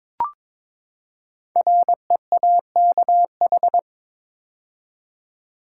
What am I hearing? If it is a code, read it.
Morse code, heard as a keyed tone, about 23 wpm, 700 Hz. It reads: REAKH
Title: Code Groups - 5 Letter:Numbers - 22wpm